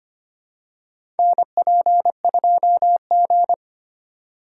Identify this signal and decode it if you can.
Morse code, heard as a keyed tone, about 25 wpm, 700 Hz. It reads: NP2G
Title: USA Call Signs - 4 Characters - 4x Character Spacing - 25wpm